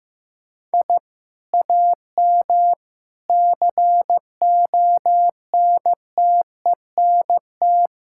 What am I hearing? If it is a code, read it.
Morse code, heard as a keyed tone, about 15 wpm, 700 Hz. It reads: I AM CONTENT
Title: Sentences from Top 1000 Words - Review Part 6 - 15wpm